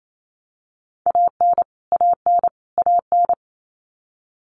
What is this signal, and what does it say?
Morse code, heard as a keyed tone, about 28 wpm, 700 Hz. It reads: AN AN AN
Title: Top 500 Words Repeated 28wpm